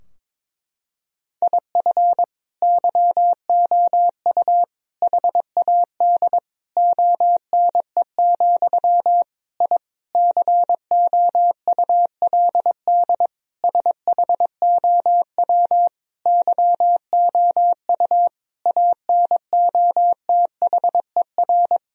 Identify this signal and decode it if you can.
Morse code, heard as a keyed tone, about 22 wpm, 700 Hz. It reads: IF YOU HAD ONE, I COULD SHOW YOU ANOTHER
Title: Sentences from Top 1000 Words - Review Part 3 - 22wpm